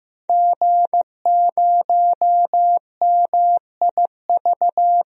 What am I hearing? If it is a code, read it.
Morse code, heard as a keyed tone, about 15 wpm, 700 Hz. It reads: G0MIV
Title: DXCC-100 Call Signs - 5 Characters - 15wpm